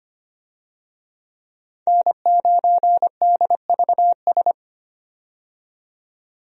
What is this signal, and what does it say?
Morse code, heard as a keyed tone, about 25 wpm, 700 Hz. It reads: N9DVS
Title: USA Call Signs - 5 Characters - 25wpm